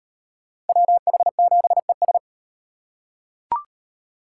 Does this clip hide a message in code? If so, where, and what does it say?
Yes: Morse code from 0.7 to 2.2 s, heard as a keyed tone, about 35 wpm, 700 Hz. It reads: WH7ES